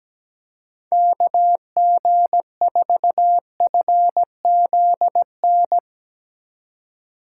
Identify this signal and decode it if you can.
Morse code, heard as a keyed tone, about 17 wpm, 700 Hz. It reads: KG4FZN
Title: Speed Racing - USA 2x3 Call Signs 17wpm